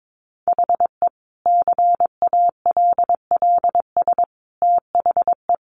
Morse code, heard as a keyed tone, about 22 wpm, 700 Hz. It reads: HE CALLS THE
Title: Additional Easy Phrases 22wpm